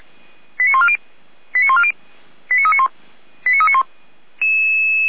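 Five-tone selective call (ZVEI), heard as quick runs of five tones, tones of about 70 ms. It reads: 89140 89140 89391 89391
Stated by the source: ZVEI five-tone call received by SDR, fire brigade Orlamünde (feuerwehr-orlamuende.org).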